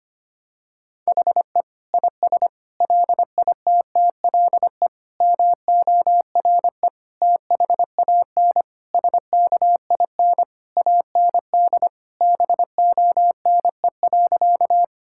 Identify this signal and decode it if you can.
Morse code, heard as a keyed tone, about 25 wpm, 700 Hz. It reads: HE IS LITTLE MORE THAN SKIN AND BONE.
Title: Sentences from Top 1600 Words 25wpm